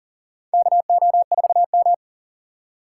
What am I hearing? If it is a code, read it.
Morse code, heard as a keyed tone, about 44 wpm, 700 Hz. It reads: KO4M
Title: USA Call Signs - 4 Characters - 4x Character Spacing - 40wpm